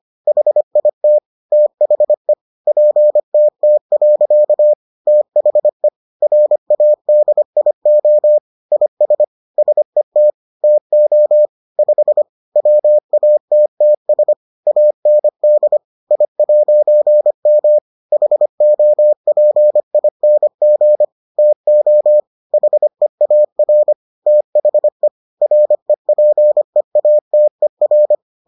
Morse code, heard as a keyed tone, about 25 wpm, 600 Hz. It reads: HIT THE PTT. THE RADIO IS SET TO 5 WATTS AND I'M HOPING TO HEAR THE REPEATER